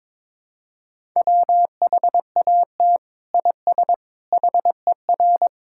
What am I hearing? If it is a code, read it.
Morse code, heard as a keyed tone, about 21 wpm, 700 Hz. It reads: WHAT IS HER
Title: Sentences from Top 1000 Words - Review Part 6 - 22wpm